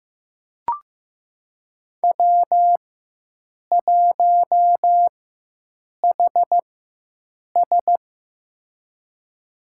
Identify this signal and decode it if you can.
Morse code, heard as a keyed tone, about 15 wpm, spaced as for 7 wpm, 700 Hz. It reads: W1HS
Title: USA Call Signs - 4 Characters - 4x Character Spacing - 15wpm